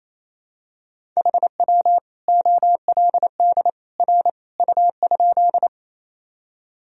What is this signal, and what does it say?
Morse code, heard as a keyed tone, about 28 wpm, 700 Hz. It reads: HW OLD R U?
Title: Sentences from Top 200 Words and Abbreviations 28wpm